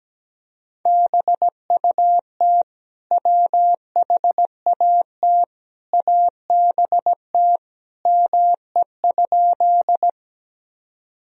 Morse code, heard as a keyed tone, about 17 wpm, 700 Hz. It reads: BUT WHAT ABT ME?
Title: Sentences from Top 200 Words and Abbreviations 17wpm